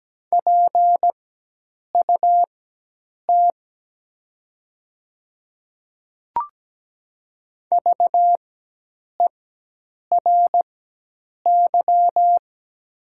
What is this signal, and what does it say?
Morse code, heard as a keyed tone, about 17 wpm, spaced as for 8 wpm, 700 Hz. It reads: PUT VERY
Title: 4x Character Spacing - Top 200 Words - 17wpm